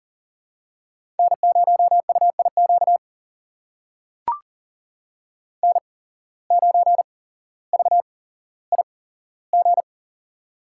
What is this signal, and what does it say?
Morse code, heard as a keyed tone, about 40 wpm, spaced as for 17 wpm, 700 Hz. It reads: N0UIQ N9VIG
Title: USA Call Signs - 5 Characters - 8x Character Spacing - 40wpm